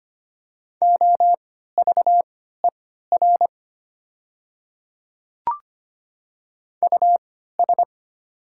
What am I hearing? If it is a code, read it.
Morse code, heard as a keyed tone, about 25 wpm, spaced as for 14 wpm, 700 Hz. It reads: OVER US
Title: 3x Character Spacing - Top 200 Words - 25wpm